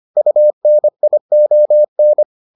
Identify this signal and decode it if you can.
Morse code, heard as a keyed tone, about 25 wpm, 600 Hz. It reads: UNION